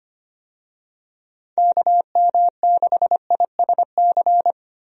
Morse code, heard as a keyed tone, about 25 wpm, 700 Hz. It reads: KM6ISC
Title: USA Call Signs - 6 Characters - Repeated 3x - 25wpm